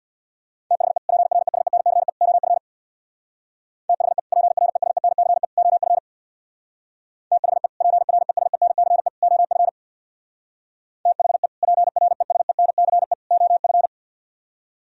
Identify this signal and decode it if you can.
Morse code, heard as a keyed tone, about 63 wpm, 700 Hz. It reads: THE PRESENCE OF THE PRESENCE OF THE PRESENCE OF THE PRESENCE OF